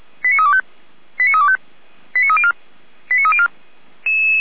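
Five-tone selective call (ZVEI), heard as quick runs of five tones, tones of about 70 ms. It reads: 89325 89325 89304 89304
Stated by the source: ZVEI five-tone selective call 89315, received by SDR on the feuerwehr-orlamuende fire brigade audio server.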